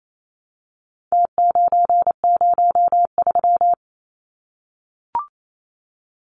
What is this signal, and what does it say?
Morse code, heard as a keyed tone, about 28 wpm, 700 Hz. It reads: T903
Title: Numbers - Contest Style - 4-Digit 28wpm